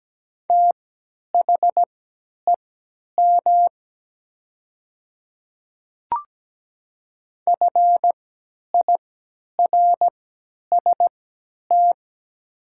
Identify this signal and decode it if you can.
Morse code, heard as a keyed tone, about 17 wpm, spaced as for 10 wpm, 700 Hz. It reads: THEM FIRST
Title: 3x Character Spacing - Top 100 Words - 17wpm